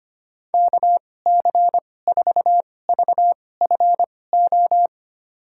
Morse code, heard as a keyed tone, about 25 wpm, spaced as for 18 wpm, 700 Hz. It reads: KC4VFO
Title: USA Call Signs - 6 Characters - 2x Character Spacing - 25wpm